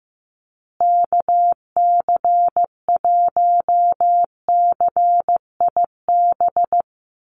Morse code, heard as a keyed tone, about 15 wpm, 700 Hz. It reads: KC1CIB
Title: USA Call Signs Repeated 3x 15wpm